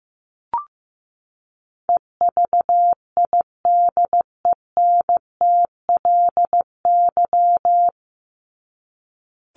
Morse code, heard as a keyed tone, about 15 wpm, 700 Hz. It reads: EVIDENTLY
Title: LY Suffix - Top 100 Words with Base Words 15wpm